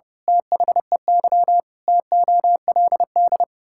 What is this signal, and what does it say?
Morse code, heard as a keyed tone, about 32 wpm, 700 Hz. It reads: THEY TOLD
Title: Pangram Challenge v2 - 30wpm